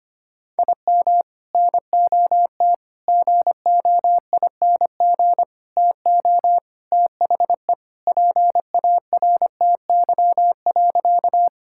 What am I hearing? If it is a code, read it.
Morse code, heard as a keyed tone, about 25 wpm, 700 Hz. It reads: IM NOT GOING TO THE PARTY.
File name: practice / 25wpm / Sentences with 3-Word Phrases - 25wpm